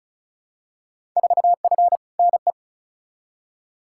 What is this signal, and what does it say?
Morse code, heard as a keyed tone, about 36 wpm, 700 Hz. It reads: 4F NE
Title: Field Day Exchanges Encore - 35wpm